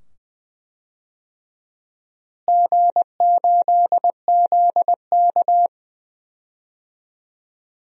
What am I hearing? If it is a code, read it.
Morse code, heard as a keyed tone, about 21 wpm, 700 Hz. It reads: G8ZK
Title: DXCC-100 Call Signs - 4 Characters - Repeated 2x - 20wpm